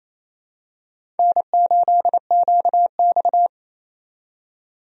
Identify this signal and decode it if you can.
Morse code, heard as a keyed tone, about 27 wpm, 700 Hz. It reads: N8QX
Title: USA Call Signs - 4 Characters - 28wpm